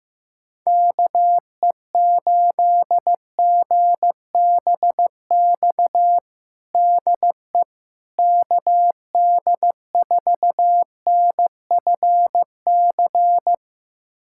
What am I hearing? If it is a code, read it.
Morse code, heard as a keyed tone, about 15 wpm, 700 Hz. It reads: KE8GBX DE KD4NFC
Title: USA Callsign DE Callsign - 6 Characters - 15wpm